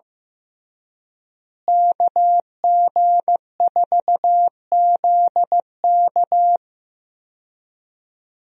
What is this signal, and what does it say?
Morse code, heard as a keyed tone, about 15 wpm, 700 Hz. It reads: KG4ZK
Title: USA Call Signs - 5 Characters - Repeated 3x - 15wpm